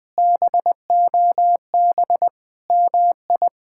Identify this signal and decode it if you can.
Morse code, heard as a keyed tone, about 21 wpm, 700 Hz. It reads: BOB MI
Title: K1USN SST QSO Practice - 20wpm